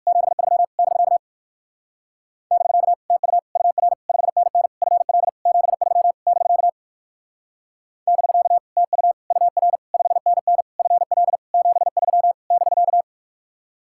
Morse code, heard as a keyed tone, about 54 wpm, 700 Hz. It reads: 73 <BK> <BK> TU UR 5NN FL 73 <BK> <BK> TU UR 5NN FL 73 <BK>
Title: Speed-Racing Common QSO Phrases - 45wpm